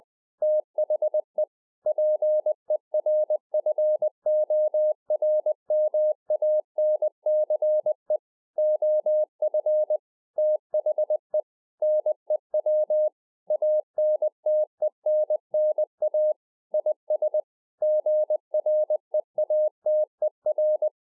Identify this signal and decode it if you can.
Morse code, heard as a keyed tone, about 20 wpm, 610 Hz. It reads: THE PERFORMANCE OF THE NEW ANTENNA IS GREATER